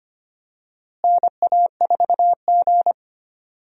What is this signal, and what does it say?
Morse code, heard as a keyed tone, about 25 wpm, 700 Hz. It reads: NA4G
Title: USA Call Signs - 4 Characters - Repeated 3x - 25wpm